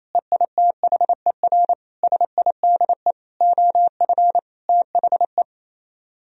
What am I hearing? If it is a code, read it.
Morse code, heard as a keyed tone, about 37 wpm, 700 Hz. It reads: EITHER SIDE OF THE